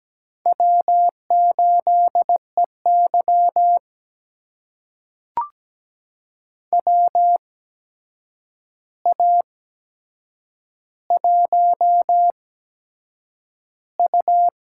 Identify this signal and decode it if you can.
Morse code, heard as a keyed tone, about 17 wpm, spaced as for 7 wpm, 700 Hz. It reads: W8EY WA1U
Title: USA Call Signs - 4 Characters - 8x Character Spacing - 17wpm